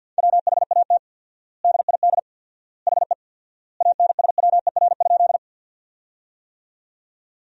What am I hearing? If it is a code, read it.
Morse code, heard as a keyed tone, about 53 wpm, 700 Hz. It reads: WHAT DID HE ANSWER?
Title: 4x Word-Spacing - Sentences from Top 300 Words - 50wpm